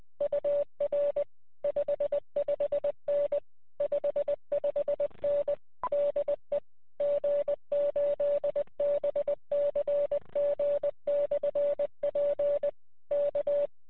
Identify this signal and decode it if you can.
Morse code, heard as a keyed tone, about 20 wpm, 580 Hz. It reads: UR 55N 55N DE G8BCG/P K
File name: 9l1ab-qso